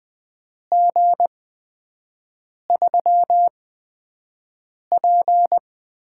Morse code, heard as a keyed tone, about 20 wpm, spaced as for 5 wpm, 700 Hz. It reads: G3P
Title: USA Call Signs - 4 Characters - 8x Character Spacing - 20wpm